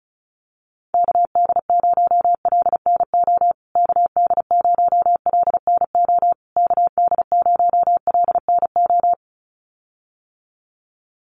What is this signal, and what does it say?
Morse code, heard as a keyed tone, about 34 wpm, 700 Hz. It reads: KD0LNO KD0LNO KD0LNO
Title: USA Call Signs Repeated 3x 35wpm